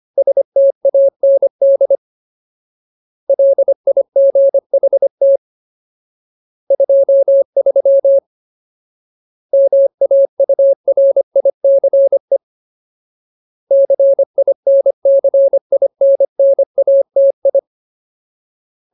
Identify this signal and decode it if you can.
Morse code, heard as a keyed tone, about 25 wpm, 550 Hz. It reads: STAND LIGHT 23 MAURICE CINCINNATI
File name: Ultimate Word List / 25WPM Ultimate